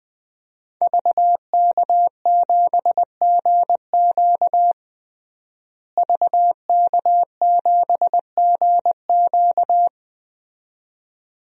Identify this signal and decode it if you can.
Morse code, heard as a keyed tone, about 20 wpm, 700 Hz. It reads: VK7GQ VK7GQ
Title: DXCC-100 Call Signs - 5 Characters - Repeated 2x - 20wpm